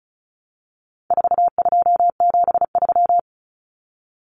Morse code, heard as a keyed tone, about 34 wpm, 700 Hz. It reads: 4273
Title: Numbers - 4-Digit 35wpm